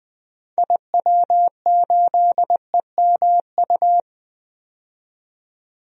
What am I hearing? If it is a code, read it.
Morse code, heard as a keyed tone, about 20 wpm, 700 Hz. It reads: IW8EMU
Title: DXCC-100 Call Signs - 6 Characters - 20wpm